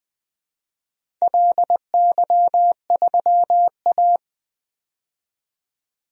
Morse code, heard as a keyed tone, about 20 wpm, 700 Hz. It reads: LY3A
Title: DXCC-100 Call Signs - 4 Characters - 20wpm